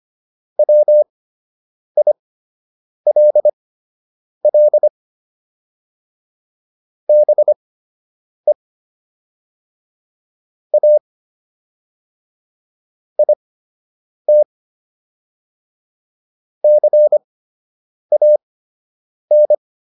Morse code, heard as a keyed tone, about 25 wpm, spaced as for 8 wpm, 600 Hz. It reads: WILL BE A IT CAN